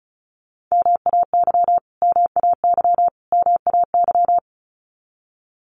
Morse code, heard as a keyed tone, about 34 wpm, 700 Hz. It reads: MAY MAY MAY
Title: Top 300 Words Repeated 35wpm